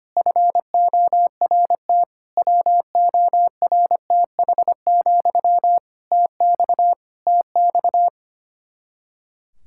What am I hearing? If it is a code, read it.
Morse code, heard as a keyed tone, about 25 wpm, 700 Hz. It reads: FORT WORTH, TX TX
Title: Practice QSO - Exchange 1 - Beginning - 25wpm